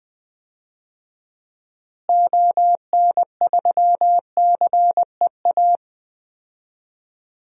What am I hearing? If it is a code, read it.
Morse code, heard as a keyed tone, about 20 wpm, 700 Hz. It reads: ON3CEA
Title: DXCC-100 Call Signs - 6 Characters - Repeated 3x - 20wpm